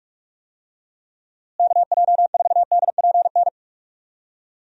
Morse code, heard as a keyed tone, about 41 wpm, 700 Hz. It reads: KJ4DPN
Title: USA Call Signs - 6 Characters - Repeated 3x - 45wpm